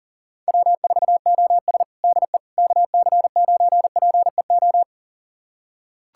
Morse code, heard as a keyed tone, about 40 wpm, 700 Hz. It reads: W4OS DE KC9PEO
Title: USA Callsign DE Callsign 40wpm